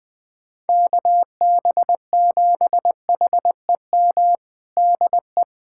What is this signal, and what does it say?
Morse code, heard as a keyed tone, about 21 wpm, 700 Hz. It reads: KB7HEM DE
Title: USA Callsign DE Callsign - 6 Characters - 20wpm